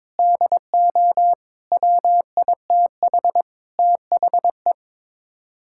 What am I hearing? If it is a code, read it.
Morse code, heard as a keyed tone, about 22 wpm, 700 Hz. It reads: DO WITH THE
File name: practice / 22wpm / Sets of 6 Words - 22wpm